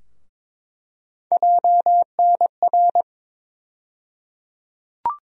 Morse code, heard as a keyed tone, about 22 wpm, 700 Hz. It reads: JNR
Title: USA Call Sign Postfixes - 3 Letters 22wpm